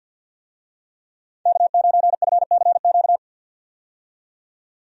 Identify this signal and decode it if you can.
Morse code, heard as a keyed tone, about 53 wpm, 700 Hz. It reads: K9FCQ
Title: USA Call Signs - 5 Characters - 50wpm